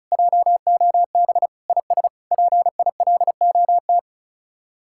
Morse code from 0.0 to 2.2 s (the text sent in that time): JOB IS